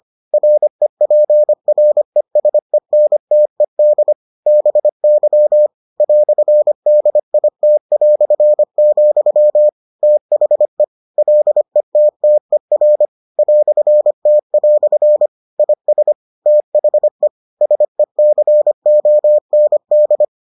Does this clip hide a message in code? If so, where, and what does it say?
Yes: Morse code from 0.3 to 20.2 s, heard as a keyed tone, about 25 wpm, 600 Hz. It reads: REPRESENTED BY "DIT", THE LETTER "T" IS THE SECOND